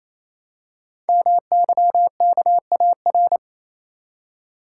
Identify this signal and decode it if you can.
Morse code, heard as a keyed tone, about 27 wpm, 700 Hz. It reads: MYKAR
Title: Code Groups - 5 Letters - 28wpm